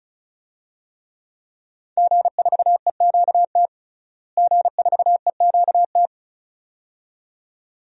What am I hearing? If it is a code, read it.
Morse code, heard as a keyed tone, about 36 wpm, 700 Hz. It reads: G4EQT G4EQT
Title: DXCC-100 Call Signs - 5 Characters - Repeated 2x - 35wpm